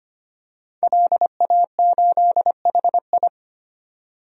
Morse code, heard as a keyed tone, about 25 wpm, 700 Hz. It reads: LA8HI
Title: DXCC-100 Call Signs - 5 Characters - 25wpm